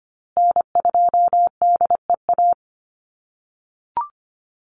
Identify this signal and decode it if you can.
Morse code, heard as a keyed tone, about 25 wpm, 700 Hz. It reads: N2DEA
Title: USA Call Signs Repeated 3x 25wpm